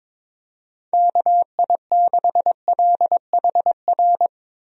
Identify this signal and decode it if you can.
Morse code, heard as a keyed tone, about 22 wpm, 700 Hz. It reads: KI6LHR
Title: USA Call Signs - 6 Characters - Repeated 3x - 22wpm